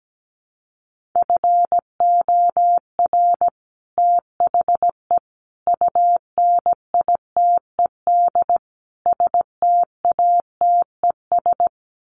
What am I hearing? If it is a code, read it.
Morse code, heard as a keyed tone, about 17 wpm, 700 Hz. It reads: FOR THE UNITED STATES
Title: Sets of 4 Words 17wpm